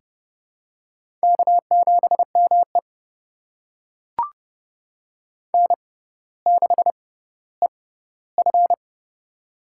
Morse code, heard as a keyed tone, about 30 wpm, spaced as for 15 wpm, 700 Hz. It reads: K7ME N6EF
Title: USA Call Signs - 4 Characters - 6x Character Spacing - 30wpm